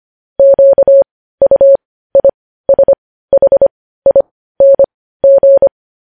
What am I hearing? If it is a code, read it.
Morse code, heard as a keyed tone, about 25 wpm, spaced as for 15 wpm, 550 Hz. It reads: QUISHING
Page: news-headlines-in-morse-code-at-15-wpm